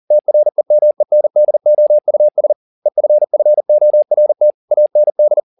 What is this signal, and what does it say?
Morse code, heard as a keyed tone, about 40 wpm, 600 Hz. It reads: TREMENDOUS EFFORT AND
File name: the old man and the sea 2 40wpm